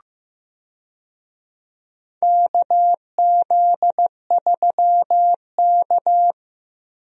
Morse code, heard as a keyed tone, about 15 wpm, 700 Hz. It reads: KZ3K
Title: USA Call Signs - 4 Characters - Repeated 3x - 15wpm